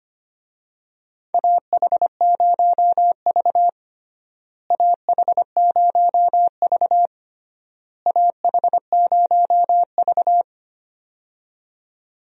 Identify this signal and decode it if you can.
Morse code, heard as a keyed tone, about 25 wpm, 700 Hz. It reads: AH0V AH0V AH0V
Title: USA Call Signs - 4 Characters - Repeated 3x - 25wpm